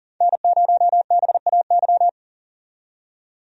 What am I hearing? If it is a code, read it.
Morse code, heard as a keyed tone, about 40 wpm, 700 Hz. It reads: N0BAY
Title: USA Call Signs - 5 Characters - 40wpm